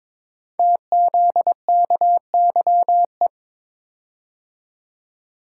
Morse code, heard as a keyed tone, about 22 wpm, 700 Hz. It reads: TZKYE
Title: Code Groups - 5 Letters - 22wpm